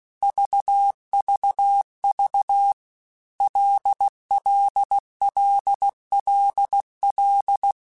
Morse code, heard as a keyed tone, about 16 wpm, 800 Hz. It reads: VVV LLLLL
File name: Basic Letter Groups By7Groups / LFCYQ / LFCYQ@15